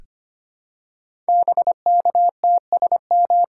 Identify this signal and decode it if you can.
Morse code, heard as a keyed tone, about 25 wpm, 700 Hz. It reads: BKTSM
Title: Code Groups - 5 Letters - 25wpm